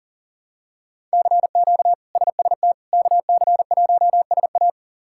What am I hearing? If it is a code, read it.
Morse code, heard as a keyed tone, about 40 wpm, 700 Hz. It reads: CQ SST KC1SA
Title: K1USN SST QSO Practice - 40wpm